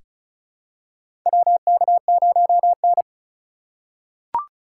Morse code, heard as a keyed tone, about 36 wpm, 700 Hz. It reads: WK0N
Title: USA Call Signs - 4 Characters - 6x Character Spacing - 35wpm